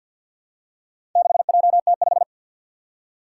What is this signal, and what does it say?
Morse code, heard as a keyed tone, about 50 wpm, 700 Hz. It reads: 6JT5
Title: Four Challenging Letter-Number Combinations 50wpm